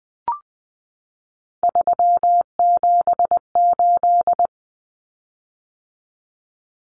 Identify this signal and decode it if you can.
Morse code, heard as a keyed tone, about 20 wpm, 700 Hz. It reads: 378
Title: Common Cut Numbers - 3-Digit 20wpm